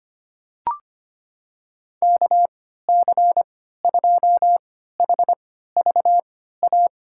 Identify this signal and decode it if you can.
Morse code, heard as a keyed tone, about 25 wpm, spaced as for 14 wpm, 700 Hz. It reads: KC2HVA
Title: USA Call Signs - 6 Characters - 3x Character Spacing - 25wpm